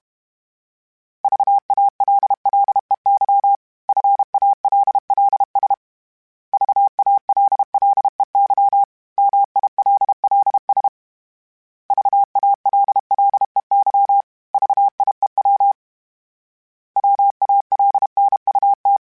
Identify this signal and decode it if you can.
Morse code, heard as a keyed tone, about 32 wpm, 800 Hz. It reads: VALLEY FALLS VALLEY MILLS VALLEY VIEW WALNUT
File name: fromSMs / FromSMs mp3s / Cities@30